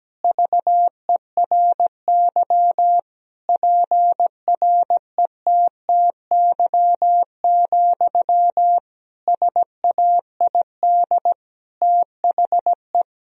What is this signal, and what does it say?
Morse code, heard as a keyed tone, about 17 wpm, 700 Hz. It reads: VERY PRETTY, SAID THE
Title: Sentences from Top 1000 Words - Review Part 4 - 17wpm